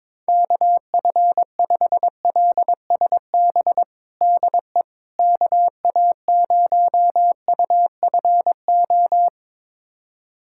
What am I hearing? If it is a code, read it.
Morse code, heard as a keyed tone, about 22 wpm, 700 Hz. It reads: KF5LSB DE KA0UFO
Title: USA Callsign DE Callsign 22wpm